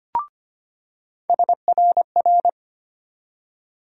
Morse code, heard as a keyed tone, about 25 wpm, 700 Hz. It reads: SRR
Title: USA Call Sign Postfixes - 3 Letters 25wpm